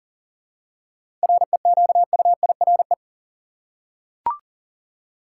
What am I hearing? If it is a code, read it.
Morse code, heard as a keyed tone, about 40 wpm, 700 Hz. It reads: REQUIRE
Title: Seven Letter Words - 40wpm